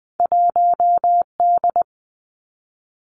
Morse code, heard as a keyed tone, about 20 wpm, 700 Hz. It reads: 1 D
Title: Two Challenging Letter-Number Combinations 20wpm